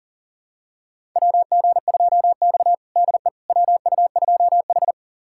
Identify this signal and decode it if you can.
Morse code, heard as a keyed tone, about 40 wpm, 700 Hz. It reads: WG2X DE WU2H
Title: USA Callsign DE Callsign - 4 Characters - 40wpm